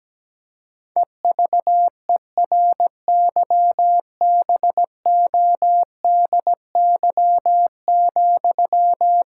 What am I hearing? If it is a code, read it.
Morse code, heard as a keyed tone, about 17 wpm, 700 Hz. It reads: EVERYBODY,
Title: Sentences from Top 1600 Words 17wpm